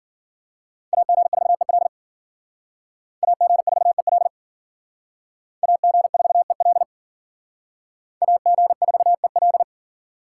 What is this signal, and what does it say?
Morse code, heard as a keyed tone, about 51 wpm, 700 Hz. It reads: AG4EL AG4EL AG4EL AG4EL